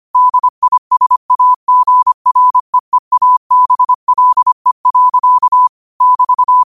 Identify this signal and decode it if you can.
Morse code, heard as a keyed tone, about 25 wpm, 1000 Hz. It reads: DISAGREEABLE. <BT>
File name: qotdfiles / 25wpm